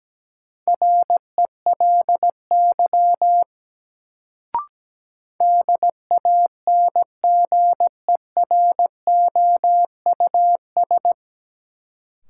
Morse code, heard as a keyed tone, about 17 wpm, 700 Hz. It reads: RELY DANGEROUS
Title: Sentences from Top 1200 Words - New Word Drill - 17wpm